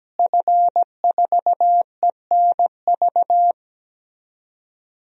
Morse code, heard as a keyed tone, about 17 wpm, 700 Hz. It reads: F4ENV
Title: DXCC-100 Call Signs - 5 Characters - Repeated 2x - 17wpm